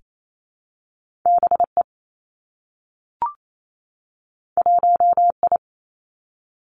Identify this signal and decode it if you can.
Morse code, heard as a keyed tone, about 28 wpm, 700 Hz. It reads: B E 1 I
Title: Two Challenging Letter-Number Combinations 28wpm